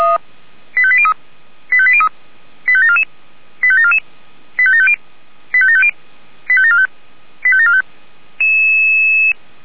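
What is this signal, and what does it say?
Five-tone selective call (ZVEI), heard as quick runs of five tones, tones of about 70 ms. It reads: 85803 85803 85744 85744 85750 85750 85745 85745
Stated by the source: ZVEI five-tone call received by SDR.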